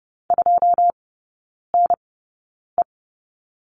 Morse code, heard as a keyed tone, about 30 wpm, spaced as for 9 wpm, 700 Hz. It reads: 2NE
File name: 081 Lesson 14e - Number 2 - Three Characters 30wpm